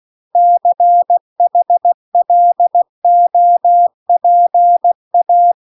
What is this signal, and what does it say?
Morse code, heard as a keyed tone, about 16 wpm, 700 Hz. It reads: CHLOPA